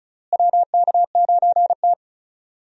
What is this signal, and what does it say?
Morse code, heard as a keyed tone, about 34 wpm, 700 Hz. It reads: WK9T
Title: USA Call Signs - 4 Characters - Repeated 2x - 35wpm